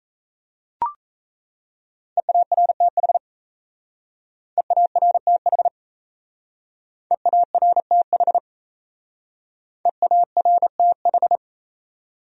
Morse code, heard as a keyed tone, about 35 wpm, 700 Hz. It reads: EARTH EARTH EARTH EARTH
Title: Speed-Racing Top 300 Words 28wpm